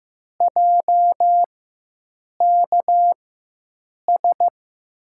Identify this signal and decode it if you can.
Morse code, heard as a keyed tone, about 15 wpm, spaced as for 7 wpm, 700 Hz. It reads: JKS
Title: USA Call Signs - 6 Characters - 4x Character Spacing - 15wpm